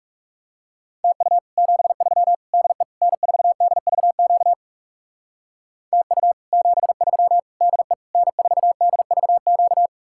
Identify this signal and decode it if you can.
Morse code, heard as a keyed tone, about 40 wpm, 700 Hz. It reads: TU 73 DE N4DVQ TU 73 DE N4DVQ